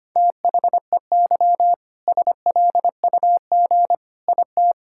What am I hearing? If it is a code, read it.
Morse code, heard as a keyed tone, about 25 wpm, 700 Hz. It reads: THEY SLUG IT
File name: practice / 25wpm / All-Age Appropriate Jokes - 25wpm